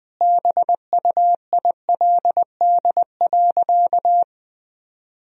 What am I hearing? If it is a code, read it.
Morse code, heard as a keyed tone, about 20 wpm, 700 Hz. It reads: BUILD.
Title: Sentences from Top 1000 Words - Review Part 1 - 20wpm